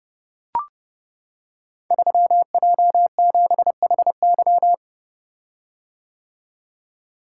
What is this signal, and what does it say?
Morse code, heard as a keyed tone, about 29 wpm, 700 Hz. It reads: 3J7HY
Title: Code Groups - 5 Letter:Numbers - 30wpm